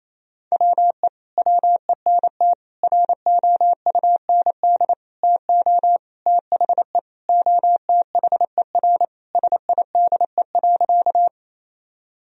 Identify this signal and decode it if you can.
Morse code, heard as a keyed tone, about 28 wpm, 700 Hz. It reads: WE WENT ROUND TO THE OTHER SIDE.
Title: Sentences from Top 1000 Words - Review Part 1 - 28wpm